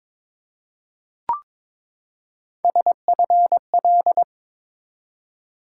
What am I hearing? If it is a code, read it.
Morse code, heard as a keyed tone, about 22 wpm, 700 Hz. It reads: SFL
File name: ARRL Sections v2 - 22wpm